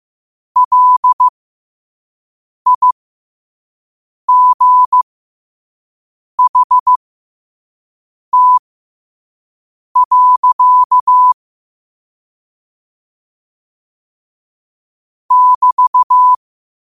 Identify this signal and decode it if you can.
Morse code, heard as a keyed tone, about 15 wpm, spaced as for 5 wpm, 1000 Hz. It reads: LIGHT. <BT>
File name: qotdfiles / 5wpm